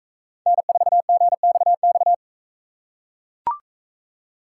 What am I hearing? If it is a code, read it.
Morse code, heard as a keyed tone, about 42 wpm, 700 Hz. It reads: N4GXX